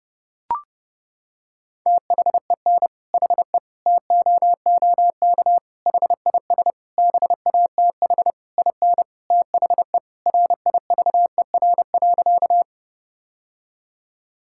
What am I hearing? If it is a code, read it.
Morse code, heard as a keyed tone, about 30 wpm, 700 Hz. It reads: THEN HE TOOK HIS BATH IN THE RIVER.
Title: Sentences from Top 1600 Words 30wpm